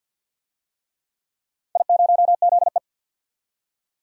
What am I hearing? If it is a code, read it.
Morse code, heard as a keyed tone, about 47 wpm, 700 Hz. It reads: I0ZE
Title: DXCC-100 Call Signs - 4 Characters - 50wpm